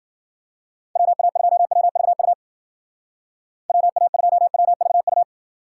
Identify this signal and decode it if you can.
Morse code, heard as a keyed tone, about 51 wpm, 700 Hz. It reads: WA2WVU WA2WVU